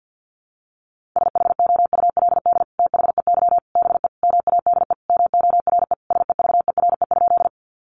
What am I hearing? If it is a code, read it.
Morse code, heard as a keyed tone, about 50 wpm, 700 Hz. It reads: SHOULD THEY BE MADE MORE SEVERE?
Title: Sentences from Top 1400 Words 50wpm